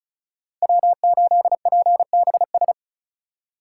Morse code, heard as a keyed tone, about 47 wpm, spaced as for 41 wpm, 700 Hz. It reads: W8PBS
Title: Speed Racing - USA 1x3 Call Signs 35wpm